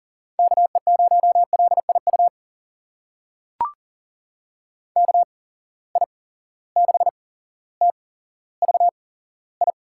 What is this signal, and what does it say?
Morse code, heard as a keyed tone, about 40 wpm, spaced as for 17 wpm, 700 Hz. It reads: KE0LIU KI6TVI
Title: USA Call Signs - 8x Character Spacing - 40wpm